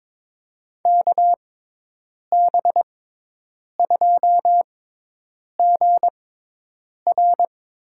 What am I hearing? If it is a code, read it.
Morse code, heard as a keyed tone, about 22 wpm, spaced as for 8 wpm, 700 Hz. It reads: KB2GR
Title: USA Call Signs - 6 Characters - 6x Character Spacing - 22wpm